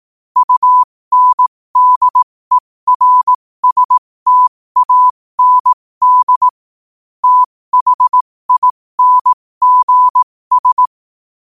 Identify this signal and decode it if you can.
Morse code, heard as a keyed tone, about 18 wpm, spaced as for 15 wpm, 1000 Hz. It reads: UNDERSTAND THINGS
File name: qotdfiles / 15wpm